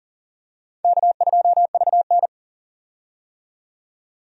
Morse code, heard as a keyed tone, about 40 wpm, 700 Hz. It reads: K2VN
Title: USA Call Signs - 4 Characters - 40wpm